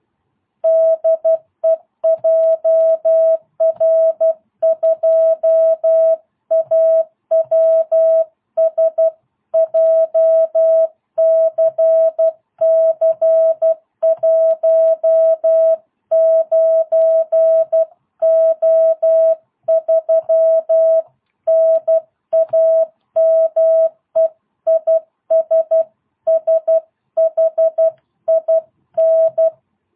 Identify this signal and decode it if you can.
Morse code, heard as a keyed tone, about 12 wpm, 660 Hz. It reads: DE JR2AWS JCC19O3 NAME IS SHIN